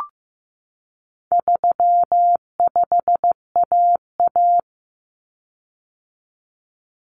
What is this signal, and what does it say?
Morse code, heard as a keyed tone, about 15 wpm, 700 Hz. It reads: 35AA
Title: Common Cut Numbers - 4-Digit 15wpm